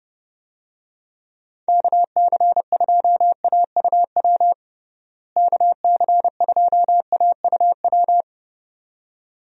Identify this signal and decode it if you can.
Morse code, heard as a keyed tone, about 30 wpm, 700 Hz. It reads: KC2AUW KC2AUW
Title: USA Call Signs - 6 Characters - Repeated 2x - 30wpm